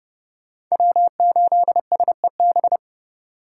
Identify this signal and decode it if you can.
Morse code, heard as a keyed tone, about 36 wpm, 700 Hz. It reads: W8SEB
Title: Speed Racing - USA 1x3 Call Signs 30wpm